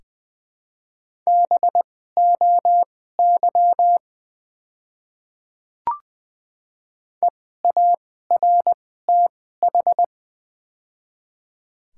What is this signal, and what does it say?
Morse code, heard as a keyed tone, about 20 wpm, spaced as for 15 wpm, 700 Hz. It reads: BOY EARTH
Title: 2x Character Spacing - Top 200 Words - 20wpm